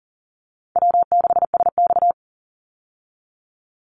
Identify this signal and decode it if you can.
Morse code, heard as a keyed tone, about 40 wpm, 700 Hz. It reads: W6SX
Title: Call Signs Repeated 3x - 40wpm